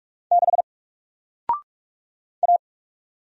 Morse code, heard as a keyed tone, about 44 wpm, 700 Hz. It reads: 6 A
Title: Single Letter-Number Alt 45wpm